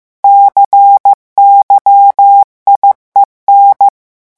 Morse code, heard as a keyed tone, about 15 wpm, 800 Hz. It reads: CYIEN
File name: morse mp3 / revision7-vitesse 14 mots